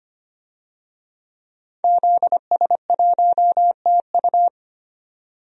Morse code, heard as a keyed tone, about 25 wpm, 700 Hz. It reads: ZS1TU
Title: DXCC-100 Call Signs - 5 Characters - 25wpm